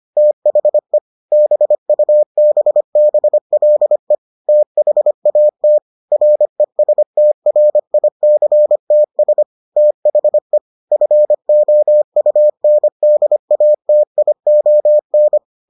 Morse code, heard as a keyed tone, about 25 wpm, 600 Hz. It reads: THE BUBBLE THAT RESTRICTS THE FOUNDATION